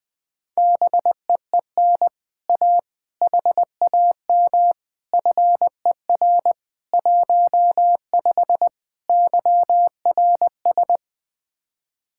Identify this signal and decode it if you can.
Morse code, heard as a keyed tone, about 20 wpm, 700 Hz. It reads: BEEN A HAM FER 15 YRS
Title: Speed-Racing Common QSO Phrases Encore - 17wpm